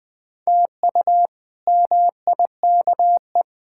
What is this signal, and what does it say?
Morse code, heard as a keyed tone, about 20 wpm, 700 Hz. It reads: TU MIKE
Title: K1USN SST QSO Practice - 20wpm